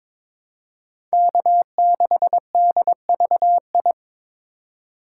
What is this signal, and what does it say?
Morse code, heard as a keyed tone, about 22 wpm, 700 Hz. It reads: K6DVI
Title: USA Call Signs - 5 Characters - 3x Character Spacing - 22wpm